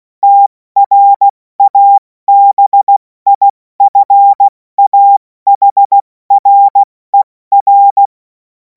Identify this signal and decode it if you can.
Morse code, heard as a keyed tone, about 16 wpm, 800 Hz. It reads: TRABIFAHRER
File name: Morsekurs / Witze / Witz 05. Mai 2026 70 F80